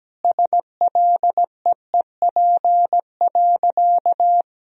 Morse code, heard as a keyed tone, about 17 wpm, 700 Hz. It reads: SLEEP.
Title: Sentences from Top 1000 Words - Review Part 1 - 17wpm